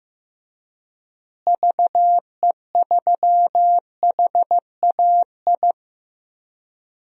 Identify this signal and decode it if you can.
Morse code, heard as a keyed tone, about 15 wpm, 700 Hz. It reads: VE3HAI
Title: DXCC-100 Call Signs - 6 Characters - Repeated 2x - 15wpm